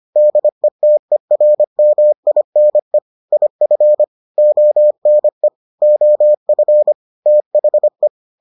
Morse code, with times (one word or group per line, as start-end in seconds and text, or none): 0.2-3.0 DETERMINE
3.3-4.0 IF
4.4-5.5 ONE
5.8-6.9 OF
7.3-8.1 THE